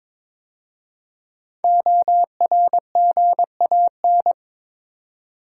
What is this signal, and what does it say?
Morse code, heard as a keyed tone, about 22 wpm, 700 Hz. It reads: ORGAN
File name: 5 Letter Words - First 3 Letters Given - 22wpm